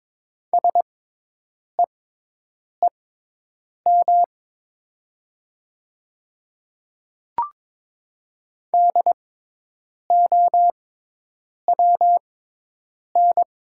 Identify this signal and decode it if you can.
Morse code, heard as a keyed tone, about 22 wpm, spaced as for 8 wpm, 700 Hz. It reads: SEEM DOWN
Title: 6x Character Spacing - Top 300 Words - 22wpm